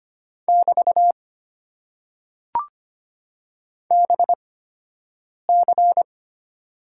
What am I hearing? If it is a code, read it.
Morse code, heard as a keyed tone, about 25 wpm, spaced as for 7 wpm, 700 Hz. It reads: <BT> BC
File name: Top 90 QSO Elements - 8x Character Spacing - 25wpm